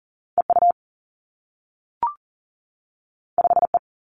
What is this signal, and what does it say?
Morse code, heard as a keyed tone, about 42 wpm, 700 Hz. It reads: E U 5 E
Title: Two Challenging Letter-Number Combinations 40wpm